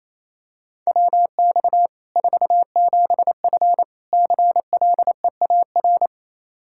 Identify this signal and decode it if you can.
Morse code, heard as a keyed tone, about 28 wpm, 700 Hz. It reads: WX 47F CLEAR
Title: Practice QSO - Exchange 2 - Info - 28wpm